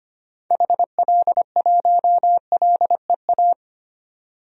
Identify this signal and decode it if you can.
Morse code, heard as a keyed tone, about 25 wpm, 700 Hz. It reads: HL1LEA
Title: DXCC-100 Call Signs - 6 Characters - 25wpm